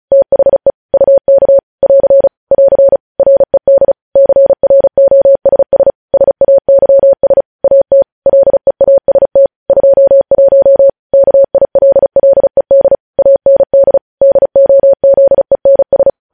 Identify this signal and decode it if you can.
Morse code, heard as a keyed tone, about 36 wpm, 550 Hz. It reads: THE UK <AR> <AR> RED CROSS SAYS AT LEAST 21 KILLED AND DOZENS
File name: podcast / News headlines-1x-35x35